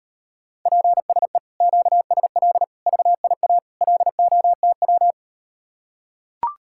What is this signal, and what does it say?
Morse code, heard as a keyed tone, about 39 wpm, 700 Hz. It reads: PSE QSL VIA LOTW